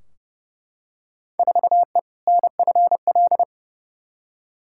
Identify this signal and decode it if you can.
Morse code, heard as a keyed tone, about 30 wpm, 700 Hz. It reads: 4E NFL
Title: Field Day Exchanges Encore - 30wpm